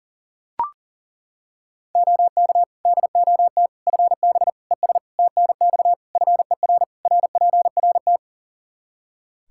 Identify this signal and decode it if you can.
Morse code, heard as a keyed tone, about 37 wpm, 700 Hz. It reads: OK DOT FB ES TNX FER RPRT
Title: Practice QSO - Exchange 2 - Info - 40wpm